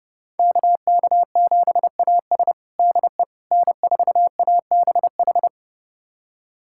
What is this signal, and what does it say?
Morse code, heard as a keyed tone, about 32 wpm, 700 Hz. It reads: KK7AS DE N4ABH
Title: USA Callsign DE Callsign - 5 Characters - 30wpm